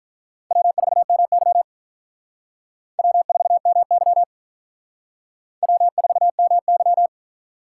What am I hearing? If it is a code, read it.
Morse code, heard as a keyed tone, about 44 wpm, 700 Hz. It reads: W4MY W4MY W4MY